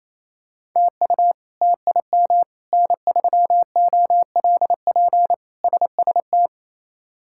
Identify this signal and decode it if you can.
Morse code, heard as a keyed tone, about 28 wpm, 700 Hz. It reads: TU TIM N3OLP SST
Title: K1USN SST QSO Practice - 28wpm